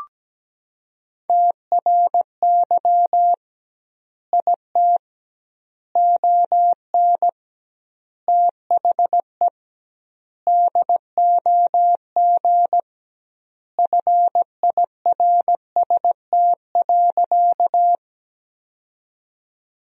Morse code, heard as a keyed tone, about 17 wpm, 700 Hz. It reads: TRY IT ON THE DOG FIRST.
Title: 2x Word-Spacing - Sentences from Top 400 Words - 17wpm